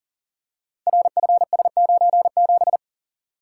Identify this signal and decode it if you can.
Morse code, heard as a keyed tone, about 38 wpm, 700 Hz. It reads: RFS97
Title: Code Groups - 5 Letter:Numbers - 40wpm